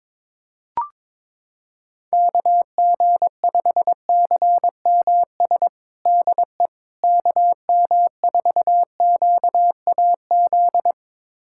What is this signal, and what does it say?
Morse code, heard as a keyed tone, about 22 wpm, 700 Hz. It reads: KG5CMS DE KM4QAZ
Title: USA Callsign DE Callsign - 6 Characters - 22wpm